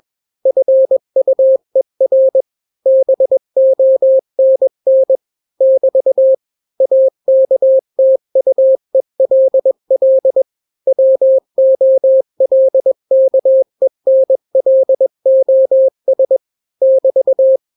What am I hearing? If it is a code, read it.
Morse code, heard as a keyed tone, about 21 wpm, 530 Hz. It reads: FUER BONN <BT> AKTUELL WOLKENLOS <BT>
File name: daily / 20wpm